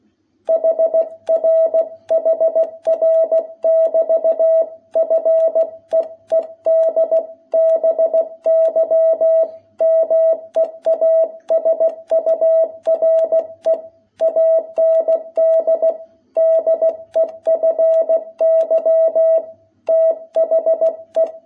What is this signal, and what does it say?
Morse code, heard as a keyed tone, about 16 wpm, 660 Hz. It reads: HR HR<BT>FEED BY MEASURE AND DEFY THE
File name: jh8cbh / 2021cw